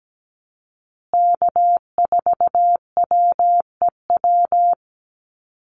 Morse code, heard as a keyed tone, about 17 wpm, 700 Hz. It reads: K4WEW
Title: USA Call Signs Repeated 2x 17wpm